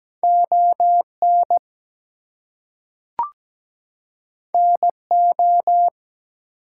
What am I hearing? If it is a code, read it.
Morse code, heard as a keyed tone, about 17 wpm, 700 Hz. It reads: ON NO
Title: Two Letter Words - 17wpm